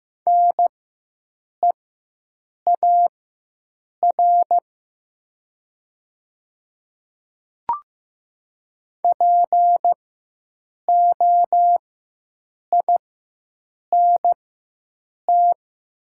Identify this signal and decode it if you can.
Morse code, heard as a keyed tone, about 15 wpm, spaced as for 7 wpm, 700 Hz. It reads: NEAR POINT
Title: 4x Character Spacing - Top 300 Words - 15wpm